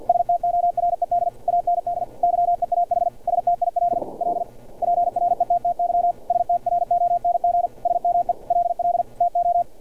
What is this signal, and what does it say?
Morse code, heard as a keyed tone, about 50 wpm, 700 Hz. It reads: STORED AND YEAH STILL PRETTY STRONG SMELL TO
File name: archived / CRSnet